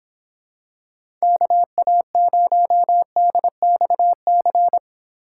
Morse code, heard as a keyed tone, about 26 wpm, 700 Hz. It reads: KA0DXC